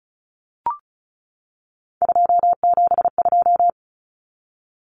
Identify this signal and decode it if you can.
Morse code, heard as a keyed tone, about 34 wpm, 700 Hz. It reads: 272
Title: Numbers - 3-Digit 35wpm